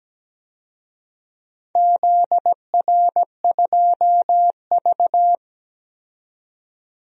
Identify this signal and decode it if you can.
Morse code, heard as a keyed tone, about 17 wpm, 700 Hz. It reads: ZR2V
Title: DXCC-100 Call Signs - 4 Characters - 17wpm